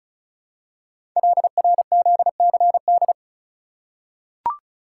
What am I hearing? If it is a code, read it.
Morse code, heard as a keyed tone, about 36 wpm, 700 Hz. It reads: LRZCD
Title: Code Groups - 5 Letters - 35wpm